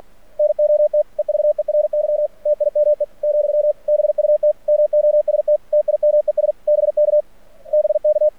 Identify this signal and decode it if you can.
Morse code, heard as a keyed tone, about 46 wpm, 600 Hz. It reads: NOT EVERY TIME , BUT MOST TIMES BK BK
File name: archived / CRSnet